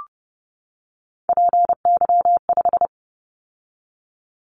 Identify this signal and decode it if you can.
Morse code, heard as a keyed tone, about 30 wpm, 700 Hz. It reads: PY5
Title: Callsign Prefixes for Top 50 DXCC - 30wpm